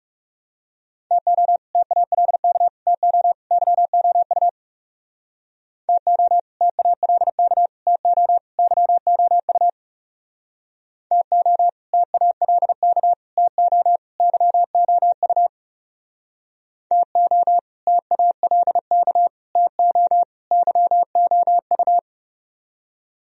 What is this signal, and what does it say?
Morse code, heard as a keyed tone, about 37 wpm, 700 Hz. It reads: TO TALK TO YOU TO TALK TO YOU TO TALK TO YOU TO TALK TO YOU
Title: Speed-Racing - Sets of 4 Words 30wpm